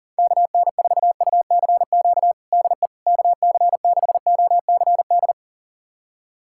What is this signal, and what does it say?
Morse code, heard as a keyed tone, about 40 wpm, 700 Hz. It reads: KN4UCQ DE KC6OCD
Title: USA Callsign DE Callsign - 6 Characters - 40wpm